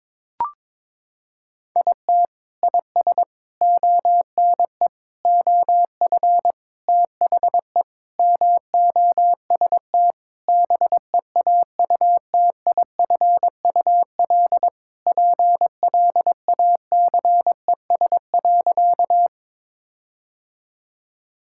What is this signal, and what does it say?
Morse code, heard as a keyed tone, about 22 wpm, 700 Hz. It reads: IT IS ONE OF THE MOST BEAUTIFUL PLACES.
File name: practice / 22wpm / Sentences with 4-Word Phrases - 22wpm